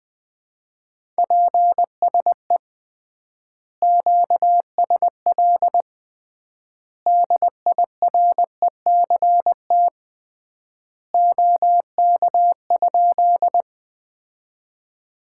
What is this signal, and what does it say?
Morse code, heard as a keyed tone, about 20 wpm, 700 Hz. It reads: PSE QSL DIRECT OK?
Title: Common QSO Phrases - 3x Word-Spacing - 20wpm